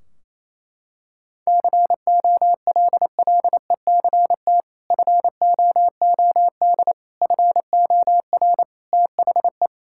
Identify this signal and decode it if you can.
Morse code, heard as a keyed tone, about 28 wpm, 700 Hz. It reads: COLLECT FOOD FOR THE
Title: Sentences from Top 1600 Words 28wpm